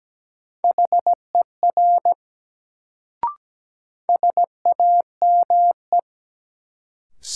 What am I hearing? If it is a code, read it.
Morse code, heard as a keyed tone, about 17 wpm, 700 Hz. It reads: HER SAME
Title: Sets of 2 Words - New Word Drill - 17wpm